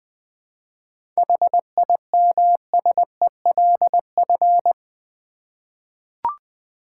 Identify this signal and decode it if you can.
Morse code, heard as a keyed tone, about 21 wpm, 700 Hz. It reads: HIMSELF
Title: Seven Letter Words - 20wpm